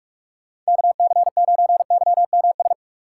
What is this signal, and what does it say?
Morse code, heard as a keyed tone, about 46 wpm, 690 Hz. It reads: KC9YMS
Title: USA Call Signs - 6 Characters - 3x Character Spacing - 45wpm